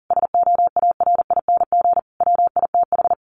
Morse code, heard as a keyed tone, about 40 wpm, 700 Hz. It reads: SOARING WITH
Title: A Sound Of Thunder 40wpm